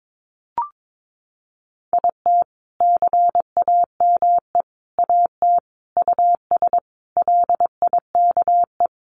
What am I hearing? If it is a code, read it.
Morse code, heard as a keyed tone, about 22 wpm, 700 Hz. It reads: IT CAME AT US LIKE
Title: Sentences from Top 1400 Words 22wpm